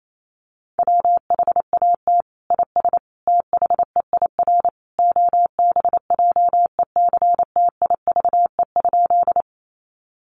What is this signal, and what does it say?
Morse code, heard as a keyed tone, about 28 wpm, 700 Hz. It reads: WHAT IS THEIR OBJECTIVE?
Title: Sentences from Top 1200 Words 28wpm